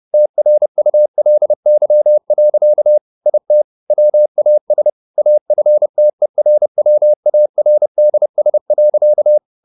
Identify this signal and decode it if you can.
Morse code, heard as a keyed tone, about 30 wpm, 600 Hz. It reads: TRULY. IT WAS AFTERWARDS.